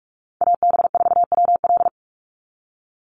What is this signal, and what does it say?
Morse code, heard as a keyed tone, about 44 wpm, 690 Hz. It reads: AB4WL
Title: NA SOTA Chaser Callsigns 45wpm